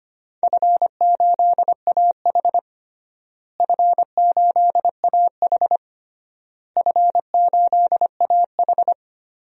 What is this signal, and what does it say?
Morse code, heard as a keyed tone, about 25 wpm, 700 Hz. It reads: F8AH F8AH F8AH
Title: DXCC-100 Call Signs - 4 Characters - Repeated 3x - 25wpm